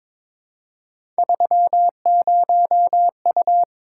Morse code, heard as a keyed tone, about 21 wpm, 700 Hz. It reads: 30U
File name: practice / 22wpm / Three Letter-Number Combinations - 22wpm